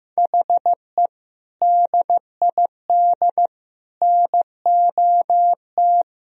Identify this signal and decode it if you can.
Morse code, heard as a keyed tone, about 15 wpm, 700 Hz. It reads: HE DID NOT
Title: Sentences from Top 1000 Words - Review Part 2 - 15wpm